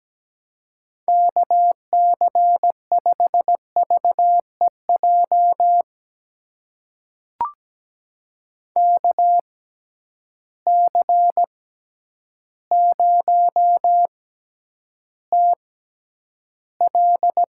Morse code, heard as a keyed tone, about 17 wpm, spaced as for 9 wpm, 700 Hz. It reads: KC5VEJ KC0TL
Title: USA Call Signs - 6 Characters - 6x Character Spacing - 17wpm